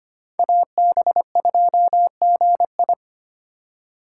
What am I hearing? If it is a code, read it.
Morse code, heard as a keyed tone, about 25 wpm, 700 Hz. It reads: AB2GI
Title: K1USN SST QSO Practice - 25wpm